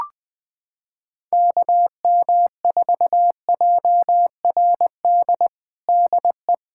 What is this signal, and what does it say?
Morse code, heard as a keyed tone, about 20 wpm, 700 Hz. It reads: KM4JRD DE
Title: USA Callsign DE Callsign 20wpm